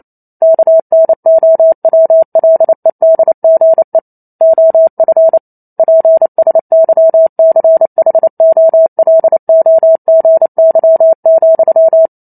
Morse code, heard as a keyed tone, about 28 wpm, 650 Hz. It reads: KNOWLEDGE OF PSYCHOLOGY,
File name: Chapter 2 - 28 WPM